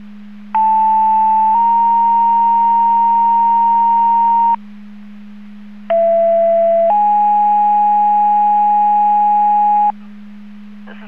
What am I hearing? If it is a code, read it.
Two-tone page, heard as a one-second tone, then a three-second tone, about 880 then 930 Hz, about 670 then 830 Hz. It is heard from 0.5 to 9.9 s.